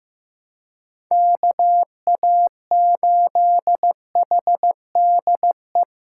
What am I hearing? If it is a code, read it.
Morse code, heard as a keyed tone, about 15 wpm, 700 Hz. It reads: KA8HDE
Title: K1USN SST QSO Practice - 15wpm